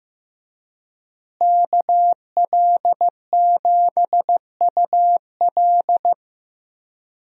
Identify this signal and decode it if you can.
Morse code, heard as a keyed tone, about 16 wpm, 700 Hz. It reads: KL7UL
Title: USA Call Signs - 5 Characters - Repeated 3x - 15wpm